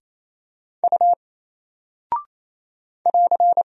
Morse code, heard as a keyed tone, about 27 wpm, 700 Hz. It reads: U <AR>
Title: Single Character (Letter:Number:Punctuation:Prosign) 28wpm